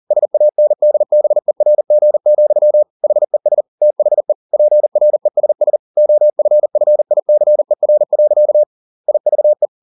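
Morse code, heard as a keyed tone, about 40 wpm, 600 Hz. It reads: SANDBERG, HES THE PRESS OFFICER. IVE
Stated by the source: Morse code at 40 wpm, practice text, CW med Gnister.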